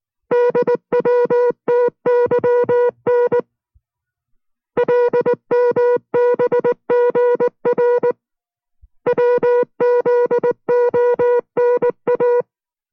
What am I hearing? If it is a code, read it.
Morse code, heard as a keyed tone, about 19 wpm, 490 Hz. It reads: DWTYN LMBGR WZONA